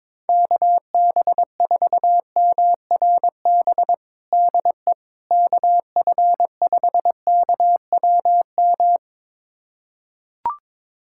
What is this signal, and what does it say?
Morse code, heard as a keyed tone, about 22 wpm, 700 Hz. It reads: KB4MRB DE KF5KWM
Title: USA Callsign DE Callsign 22wpm